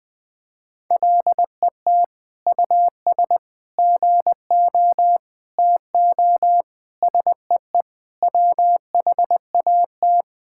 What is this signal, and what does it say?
Morse code, heard as a keyed tone, about 20 wpm, 700 Hz. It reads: LET US GO TO SEE WHAT
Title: Sentences with 8-Word Phrases - 20wpm